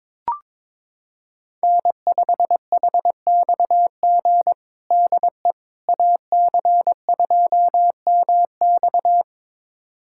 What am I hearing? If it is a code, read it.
Morse code, heard as a keyed tone, about 22 wpm, 700 Hz. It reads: N5HXG DE AC2MX
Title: USA Callsign DE Callsign - 5 Characters - 22wpm